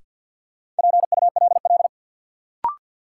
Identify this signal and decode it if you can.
Morse code, heard as a keyed tone, about 53 wpm, 700 Hz. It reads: PULL
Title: Sentences from Top 400 Words - New Word Drill - 50wpm